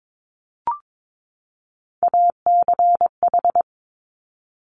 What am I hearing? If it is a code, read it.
Morse code, heard as a keyed tone, about 22 wpm, 700 Hz. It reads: ACH
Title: Top Three Letter Combinations in Words 22wpm